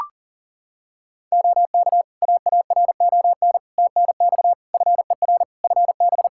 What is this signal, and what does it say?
Morse code, heard as a keyed tone, about 37 wpm, 700 Hz. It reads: OK AARON TNX FER FB
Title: K1USN SST QSO Practice 40wpm